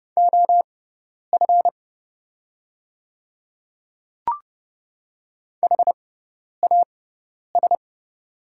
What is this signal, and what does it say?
Morse code, heard as a keyed tone, about 30 wpm, spaced as for 10 wpm, 690 Hz. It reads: OF HAS
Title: 6x Character Spacing - Top 100 Words - 30wpm